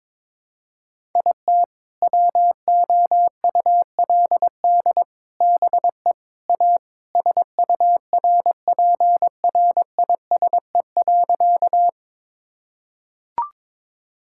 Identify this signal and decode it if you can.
Morse code, heard as a keyed tone, about 22 wpm, 700 Hz. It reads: IT WOULD BE A SURPRISE.
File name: practice / 22wpm / Sentences with 4-Word Phrases - 22wpm